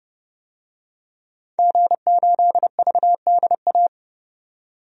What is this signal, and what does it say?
Morse code, heard as a keyed tone, about 30 wpm, 700 Hz. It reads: G8VDA
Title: DXCC-100 Call Signs - 5 Characters - 30wpm